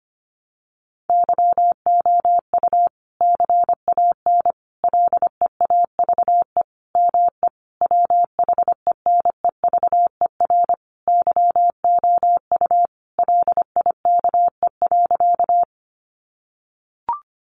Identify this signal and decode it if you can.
Morse code, heard as a keyed tone, about 25 wpm, 700 Hz. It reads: YOU CAN LEAVE ME WHENEVER YOU LIKE.
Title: Sentences from Top 1200 Words 25wpm